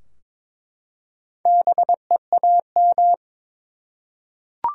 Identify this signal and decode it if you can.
Morse code, heard as a keyed tone, about 22 wpm, 700 Hz. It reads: BEAM
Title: Top 90 QSO Elements 22wpm